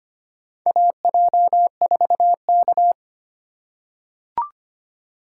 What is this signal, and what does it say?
Morse code, heard as a keyed tone, about 25 wpm, 700 Hz. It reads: AJ4K
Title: USA Call Signs - 4 Characters - 8x Character Spacing - 25wpm